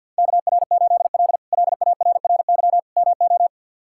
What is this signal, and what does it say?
Morse code, heard as a keyed tone, about 52 wpm, 700 Hz. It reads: KR8L LARRY MO
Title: K1USN SST QSO Practice - 50wpm